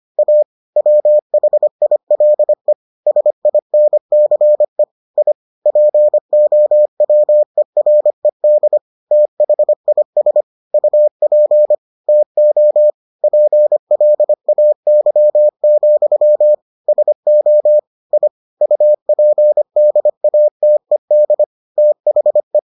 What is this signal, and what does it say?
Morse code, heard as a keyed tone, about 25 wpm, 600 Hz. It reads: A WHILE SINCE I POWERED THIS UP TO PLAY, SO I UPDATED THE